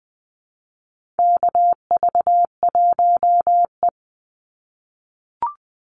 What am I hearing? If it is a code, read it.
Morse code, heard as a keyed tone, about 20 wpm, 700 Hz. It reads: KV1E
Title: Call Signs - 20wpm